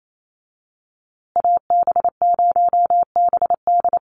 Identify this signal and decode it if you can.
Morse code, heard as a keyed tone, about 28 wpm, 700 Hz. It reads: AB0BD
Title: USA Call Signs Repeated 2x 28wpm